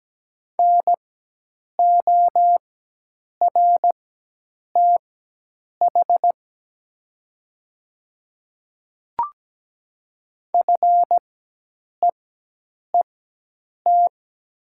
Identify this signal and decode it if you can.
Morse code, heard as a keyed tone, about 17 wpm, spaced as for 8 wpm, 700 Hz. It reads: NORTH FEET
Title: 4x Character Spacing - Top 300 Words - 17wpm